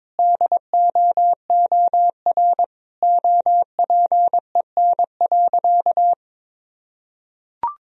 Morse code, heard as a keyed tone, about 22 wpm, 700 Hz. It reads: DOOR OPEN.
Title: Sentences from Top 1000 Words - Review Part 3 - 22wpm